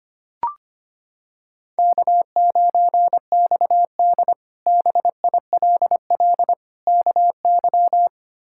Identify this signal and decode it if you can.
Morse code, heard as a keyed tone, about 25 wpm, 700 Hz. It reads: K9XD BILL KY
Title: K1USN SST QSO Practice - 25wpm